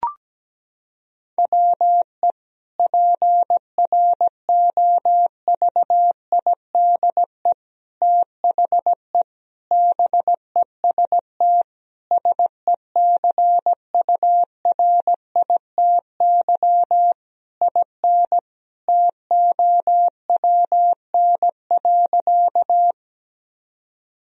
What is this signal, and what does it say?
Morse code, heard as a keyed tone, about 17 wpm, 700 Hz. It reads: WE PROVIDE THE BEST SECURITY IN TOWN.
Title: Sentences from Top 1000 Words - Review Part 4 - 17wpm